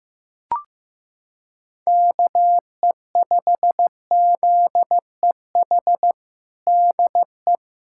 Morse code, heard as a keyed tone, about 16 wpm, 700 Hz. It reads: KE5ZEH DE
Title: USA Callsign DE Callsign 15wpm